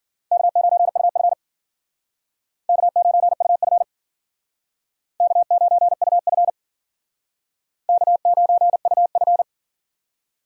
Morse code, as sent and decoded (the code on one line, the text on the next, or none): -.- ----. ..- ..-. / -.- ----. ..- ..-. / -.- ----. ..- ..-. / -.- ----. ..- ..-.
K9UF K9UF K9UF K9UF